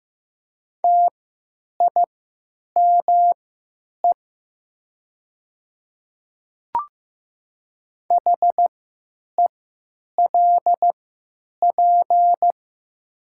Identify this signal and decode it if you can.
Morse code, heard as a keyed tone, about 15 wpm, spaced as for 9 wpm, 700 Hz. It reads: TIME HELP
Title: 3x Character Spacing - Top 300 Words - 15wpm